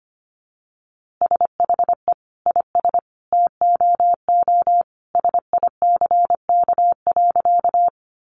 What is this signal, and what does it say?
Morse code, heard as a keyed tone, about 25 wpm, 700 Hz. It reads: SHE IS TOO SICK.
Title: Easy Phrases 25wpm